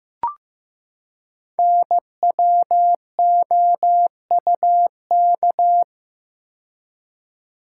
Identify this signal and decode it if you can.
Morse code, heard as a keyed tone, about 15 wpm, 700 Hz. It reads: NWOUK
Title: Code Groups - 5 Letters - 15wpm